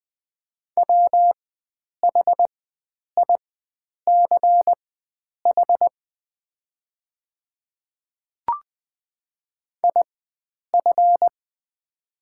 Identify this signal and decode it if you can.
Morse code, heard as a keyed tone, about 20 wpm, spaced as for 9 wpm, 700 Hz. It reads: WHICH IF
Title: 4x Character Spacing - Top 100 Words - 20wpm